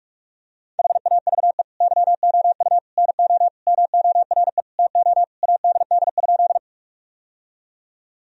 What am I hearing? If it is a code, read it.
Morse code, heard as a keyed tone, about 46 wpm, 700 Hz. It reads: HAVE YOU NO MORE TO ADD?
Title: Sentences from Top 1000 Words - Review Part 1 - 45wpm